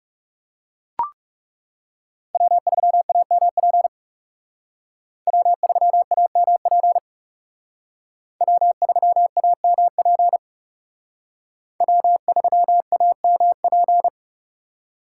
Morse code, heard as a keyed tone, about 38 wpm, 700 Hz. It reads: W3AMP W3AMP W3AMP W3AMP